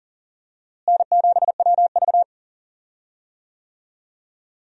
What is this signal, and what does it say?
Morse code, heard as a keyed tone, about 40 wpm, 700 Hz. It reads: N7WV
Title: USA Call Signs - 4 Characters - Repeated 2x - 40wpm